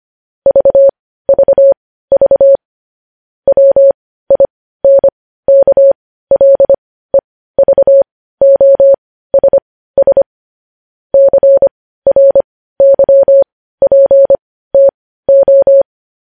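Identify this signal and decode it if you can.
Morse code, heard as a keyed tone, about 25 wpm, spaced as for 15 wpm, 550 Hz. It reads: VVV WINKLEVOSS CRYPTO
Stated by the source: Morse news headlines sent at 15 words per minute.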